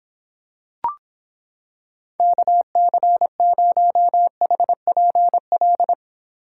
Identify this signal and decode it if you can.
Morse code, heard as a keyed tone, about 26 wpm, 700 Hz. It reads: KC0HPL